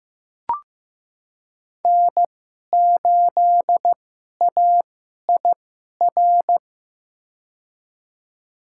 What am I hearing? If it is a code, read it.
Morse code, heard as a keyed tone, about 15 wpm, spaced as for 11 wpm, 700 Hz. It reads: N8AIR
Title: USA Call Signs - 5 Characters - 2x Character Spacing - 15wpm